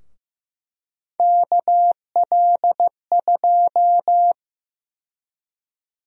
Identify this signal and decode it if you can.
Morse code, heard as a keyed tone, about 15 wpm, 700 Hz. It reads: KL2
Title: USA Call Sign Prefixes - 15wpm